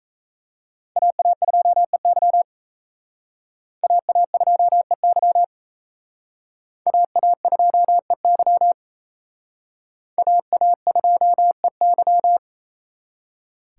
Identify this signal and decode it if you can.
Morse code, heard as a keyed tone, about 36 wpm, 700 Hz. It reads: AA2EY AA2EY AA2EY AA2EY